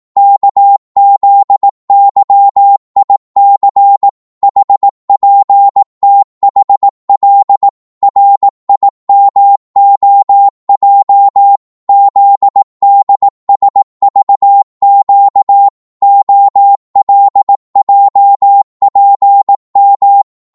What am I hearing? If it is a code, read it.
Morse code, heard as a keyed tone, about 18 wpm, 800 Hz. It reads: KZYIC HPTHL RIMOJ ZDSVQ OLJPM
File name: Morse / ltr-18wpm